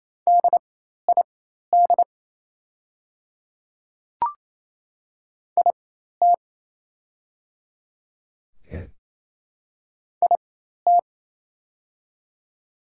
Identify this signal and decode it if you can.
Morse code, heard as a keyed tone, about 27 wpm, spaced as for 13 wpm, 690 Hz. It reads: DID IT IT
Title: 4x Character Spacing - Top 200 Words - 28wpm